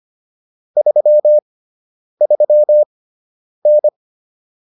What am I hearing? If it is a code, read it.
Morse code, heard as a keyed tone, about 25 wpm, spaced as for 9 wpm, 600 Hz. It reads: 33N